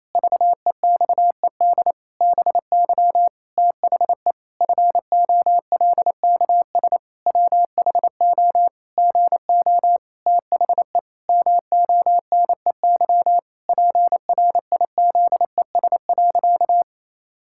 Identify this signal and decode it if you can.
Morse code, heard as a keyed tone, about 28 wpm, 700 Hz. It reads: VEXED BY THE FOLKS WHO GO THE MONEY PRIZES.
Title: Pangram Challenge v2 - 28wpm